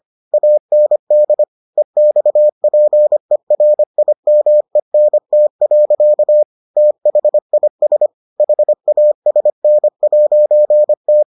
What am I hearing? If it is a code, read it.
Morse code, heard as a keyed tone, about 25 wpm, 600 Hz. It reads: AND EXPERIMENT. THIS HASN'T